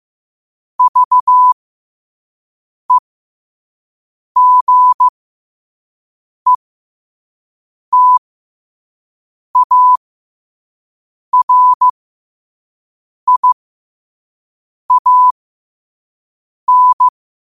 Morse code, heard as a keyed tone, about 15 wpm, spaced as for 5 wpm, 1000 Hz. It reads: VEGETARIAN
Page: New quotes every day in morse code at 5 Words per minute.